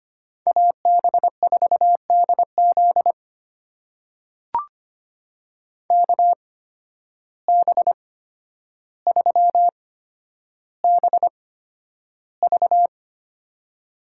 Morse code, heard as a keyed tone, about 25 wpm, spaced as for 11 wpm, 700 Hz. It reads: AB4DZ KB3BV
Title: USA Call Signs - 5 Characters - 8x Character Spacing - 25wpm